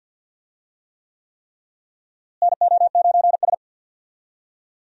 Morse code, heard as a keyed tone, about 51 wpm, 700 Hz. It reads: NO9S
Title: USA Call Signs - 4 Characters - Repeated 3x - 50wpm